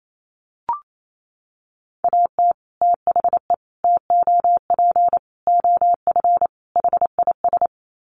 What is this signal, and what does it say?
Morse code, heard as a keyed tone, about 28 wpm, 700 Hz. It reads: AT THE TOP OF HIS
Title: Sets of 5 Words Encore 28wpm